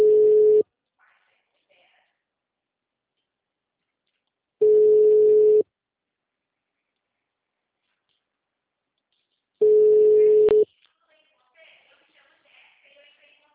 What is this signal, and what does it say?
Ringback tone (Europe, 425 Hz), with speech.